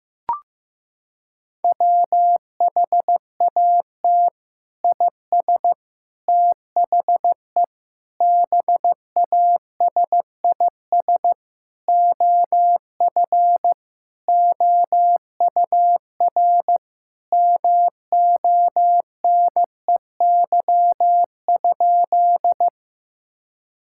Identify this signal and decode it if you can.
Morse code, heard as a keyed tone, about 15 wpm, 700 Hz. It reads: WHAT IS THE BASIS OF OUR MONEY?
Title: Sentences from Top 1000 Words - Review Part 5 - 15wpm